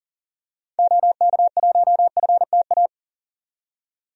Morse code, heard as a keyed tone, about 40 wpm, 700 Hz. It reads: OK1FTA
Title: DXCC-100 Call Signs - 6 Characters - 40wpm